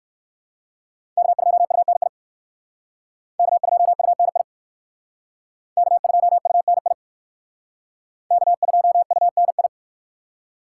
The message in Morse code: K2UNI K2UNI K2UNI K2UNI

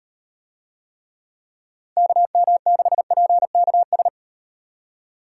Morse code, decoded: KM6PKS

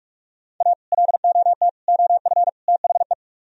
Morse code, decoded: A LOT OF THE

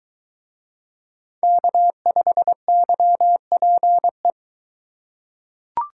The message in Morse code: K5YPE

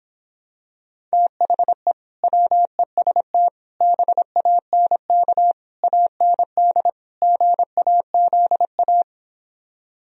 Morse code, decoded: THE WEST BANK AND GAZA

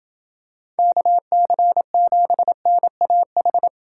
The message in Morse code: KC7NAH

